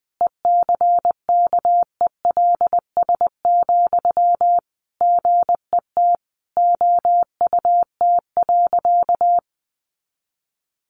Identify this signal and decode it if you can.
Morse code, heard as a keyed tone, about 20 wpm, 700 Hz. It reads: ECKELS, GET OUT.
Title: A Sound Of Thunder 20wpm